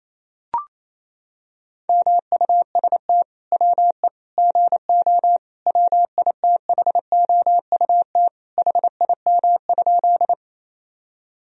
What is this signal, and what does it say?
Morse code, heard as a keyed tone, about 28 wpm, 700 Hz. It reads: MUST WE GO WITHOUT HIM?
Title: Sentences from Top 1000 Words - Review Part 3 - 28wpm